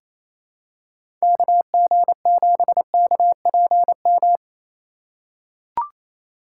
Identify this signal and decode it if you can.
Morse code, heard as a keyed tone, about 28 wpm, 700 Hz. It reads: KG7KPM
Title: USA Call Signs - 6 Characters - 3x Character Spacing - 28wpm